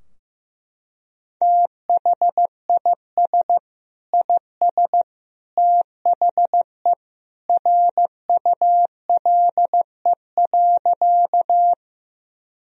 Morse code, heard as a keyed tone, about 16 wpm, 700 Hz. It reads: THIS IS THE RULE.
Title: Sentences from Top 1000 Words - Review Part 2 - 15wpm